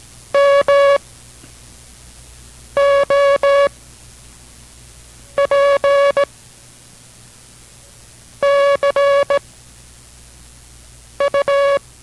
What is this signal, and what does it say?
Morse code, heard as a keyed tone, about 17 wpm, 560 Hz. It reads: M O P C U